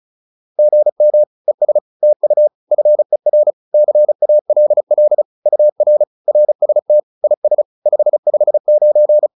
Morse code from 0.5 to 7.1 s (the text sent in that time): GM ES TU FER CALL UR RST